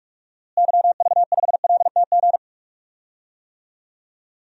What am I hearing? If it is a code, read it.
Morse code, heard as a keyed tone, about 46 wpm, 700 Hz. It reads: YV5LTG
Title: DXCC-100 Call Signs - 6 Characters - Repeated 3x - 45wpm